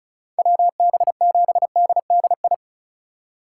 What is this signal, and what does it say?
Morse code, heard as a keyed tone, about 34 wpm, 700 Hz. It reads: WB7DDI
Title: USA Call Signs - 3x Character Spacing - 35wpm